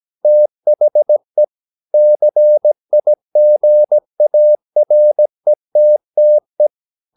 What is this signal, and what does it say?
Morse code, heard as a keyed tone, about 17 wpm, 600 Hz. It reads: THE CIGARETTE
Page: Romanen Never af Ken Follet på engelsk.